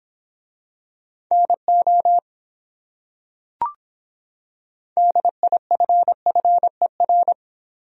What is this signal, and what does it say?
Morse code, heard as a keyed tone, about 26 wpm, 700 Hz. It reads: NO DIFFER